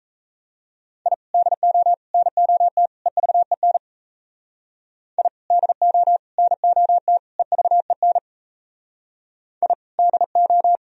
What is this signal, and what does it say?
Morse code, heard as a keyed tone, about 35 wpm, 700 Hz. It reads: I DO NOT EVEN I DO NOT EVEN I DO
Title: Speed-Racing - Sets of 4 Words 28wpm